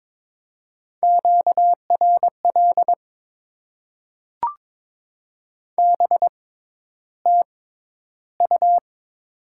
Morse code, heard as a keyed tone, about 22 wpm, spaced as for 11 wpm, 700 Hz. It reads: QRL BTU
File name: Top 20 QSO Elements - 6x Character Spacing - 22wpm